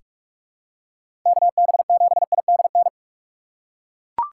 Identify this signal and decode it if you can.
Morse code, heard as a keyed tone, about 41 wpm, 700 Hz. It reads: KB7IDN
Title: USA Call Signs - 6 Characters - 4x Character Spacing - 45wpm